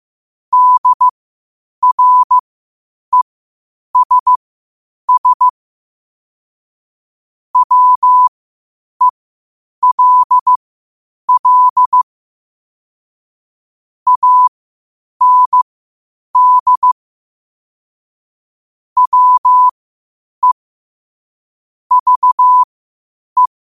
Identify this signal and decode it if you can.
Morse code, heard as a keyed tone, about 15 wpm, spaced as for 8 wpm, 1000 Hz. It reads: DRESS WELL AND WE VE
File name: 8wpm